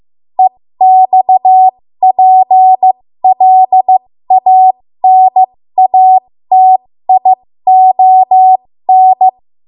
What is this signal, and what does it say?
Morse code, heard as a keyed tone, about 15 wpm, spaced as for 13 wpm, 750 Hz. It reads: EXPLANATION
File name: Morse / 13WPM